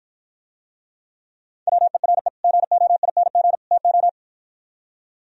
Morse code, heard as a keyed tone, about 52 wpm, 700 Hz. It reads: WERE GOING TO